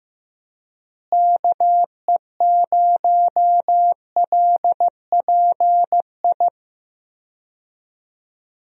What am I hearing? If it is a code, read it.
Morse code, heard as a keyed tone, about 15 wpm, 700 Hz. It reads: KE0LPI
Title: USA Call Signs - 6 Characters - Repeated 3x - 15wpm